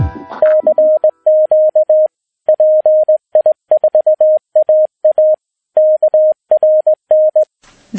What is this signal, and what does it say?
Morse code, heard as a keyed tone, about 19 wpm, 620 Hz. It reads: CQ PI4AA KRN
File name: pi4aa / PI4AA Uitzending